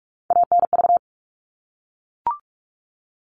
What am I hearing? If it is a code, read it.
Morse code, heard as a keyed tone, about 44 wpm, 700 Hz. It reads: ANV
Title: All Cut Numbers - 3-Digit 45wpm